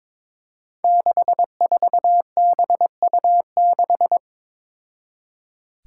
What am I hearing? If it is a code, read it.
Morse code, heard as a keyed tone, about 21 wpm, 700 Hz. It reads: 64BU6
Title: Five Challenging Letter-Number Combinations 22wpm